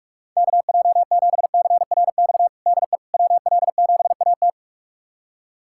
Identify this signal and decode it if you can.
Morse code, heard as a keyed tone, about 41 wpm, 700 Hz. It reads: KJ7CRX DE WL7AT
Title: USA Callsign DE Callsign 45wpm